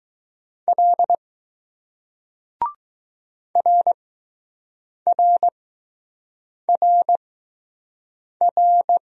Morse code, heard as a keyed tone, about 20 wpm, spaced as for 6 wpm, 700 Hz. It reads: L RRRR